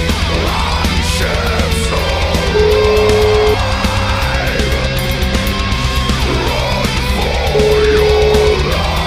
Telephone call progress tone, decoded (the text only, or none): ringback tone